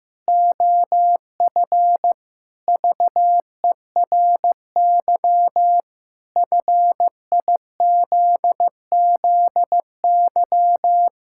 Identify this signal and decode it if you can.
Morse code, heard as a keyed tone, about 15 wpm, 700 Hz. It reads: OF VERY FIZZY
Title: Pangram Challenge v2 - 15wpm